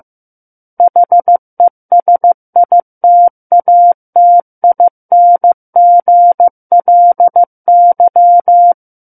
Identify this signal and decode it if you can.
Morse code, heard as a keyed tone, about 15 wpm, 700 Hz. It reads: HESITATINGLY
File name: Book One / 06-Chapter 6 - 15 WPM